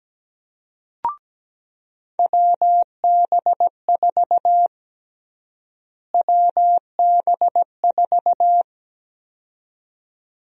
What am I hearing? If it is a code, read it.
Morse code, heard as a keyed tone, about 17 wpm, 700 Hz. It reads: WB4 WB4
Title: USA Call Sign Prefixes - Repeated 2x - 17wpm